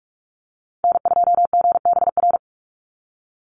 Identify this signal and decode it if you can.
Morse code, heard as a keyed tone, about 44 wpm, 700 Hz. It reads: N2GBR
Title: NA SOTA Chaser Callsigns 45wpm